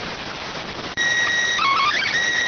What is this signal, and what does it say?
SSTV (slow-scan television), PD 290.